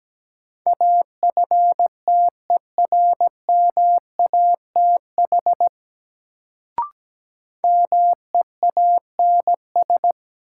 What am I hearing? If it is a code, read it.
Morse code, heard as a keyed tone, about 17 wpm, 700 Hz. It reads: AFTERMATH MEANS
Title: Sets of 5 Words - New Word Drill - 17wpm